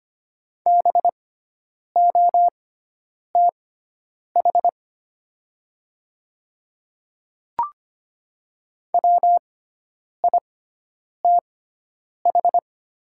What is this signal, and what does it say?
Morse code, heard as a keyed tone, about 25 wpm, spaced as for 9 wpm, 700 Hz. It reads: BOTH WITH
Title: 6x Character Spacing - Top 500 Words - 25wpm